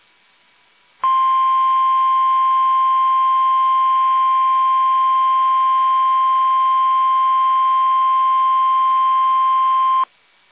Emergency Alert System (SAME) tones (the 1050 Hz attention tone). It is heard from 1.0 to 10.1 s.
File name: eas / KUSO / KUSO